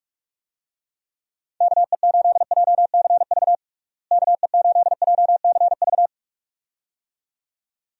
Morse code, heard as a keyed tone, about 46 wpm, 700 Hz. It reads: KE8JCV KE8JCV
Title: USA Call Signs - 6 Characters - Repeated 2x - 45wpm